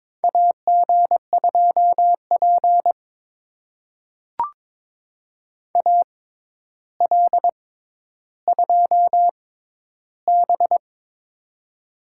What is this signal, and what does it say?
Morse code, heard as a keyed tone, about 22 wpm, spaced as for 11 wpm, 700 Hz. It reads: AG2P AL2B
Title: USA Call Signs - 4 Characters - 6x Character Spacing - 22wpm